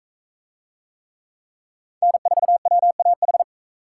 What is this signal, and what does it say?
Morse code, heard as a keyed tone, about 36 wpm, 700 Hz. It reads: N4WAH